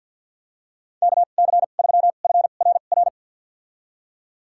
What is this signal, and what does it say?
Morse code, heard as a keyed tone, about 51 wpm, spaced as for 36 wpm, 700 Hz. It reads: KC3FRR
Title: USA Call Signs - 2x Character Spacing - 50wpm